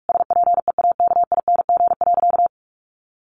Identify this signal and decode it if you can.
Morse code, heard as a keyed tone, about 44 wpm, 700 Hz. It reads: SPEAKING.
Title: A Sound Of Thunder 45wpm